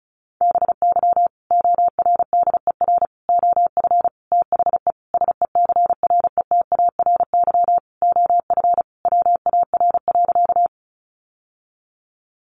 Morse code, heard as a keyed tone, about 34 wpm, 700 Hz. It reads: BY ORDER OF THE SECRETARY OF WAR.
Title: Sentences from Top 1300 Words 35wpm